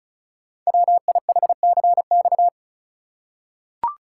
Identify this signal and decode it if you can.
Morse code, heard as a keyed tone, about 36 wpm, 700 Hz. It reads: WIHCX
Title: Code Groups - 5 Letters - 35wpm